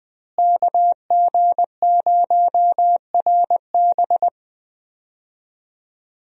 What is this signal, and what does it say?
Morse code, heard as a keyed tone, about 20 wpm, 700 Hz. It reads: KG0RB
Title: USA Call Signs - 5 Characters - Repeated 3x - 20wpm